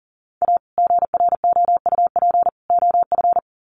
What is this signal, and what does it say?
Morse code, heard as a keyed tone, about 40 wpm, 700 Hz. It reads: A GROUP OF
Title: Sets of 3 Words 40wpm